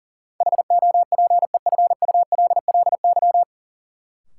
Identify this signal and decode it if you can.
Morse code, heard as a keyed tone, about 40 wpm, 700 Hz. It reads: HOPEFULLY
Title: Sentences from Top 1500 Words - New Word Drill - 40wpm